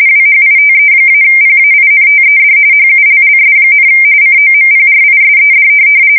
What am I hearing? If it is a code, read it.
RTTY, about 45.45 baud. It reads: WHITE-PAINTED HULLS, COMPRISED
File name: fsk170 / USS-New-Jersey